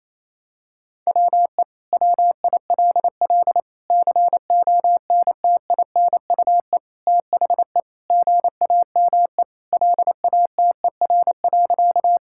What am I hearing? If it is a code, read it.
Morse code, heard as a keyed tone, about 28 wpm, 700 Hz. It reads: WE WILL CONTINUE THE GAME LATER.
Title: Sentences from Top 1000 Words - Review Part 6 - 28wpm